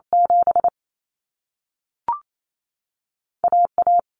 Morse code, heard as a keyed tone, about 28 wpm, 700 Hz. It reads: 7 AA
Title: Common Cut Numbers - 2-Digit 28wpm